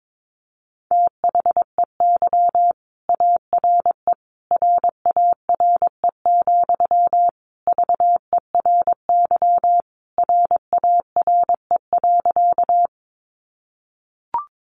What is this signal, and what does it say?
Morse code, heard as a keyed tone, about 22 wpm, 700 Hz. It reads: THEY ARE RARE, VERY RARE.
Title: Sentences from Top 1100 Words 22wpm